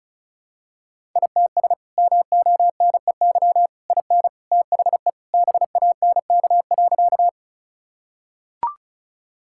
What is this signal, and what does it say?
Morse code, heard as a keyed tone, about 36 wpm, 700 Hz. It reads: ITS MONEY IN THE BANK.
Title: Sentences from Top 1000 Words - Review Part 6 - 35wpm